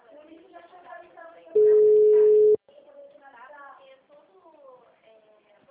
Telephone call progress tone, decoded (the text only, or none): ringback tone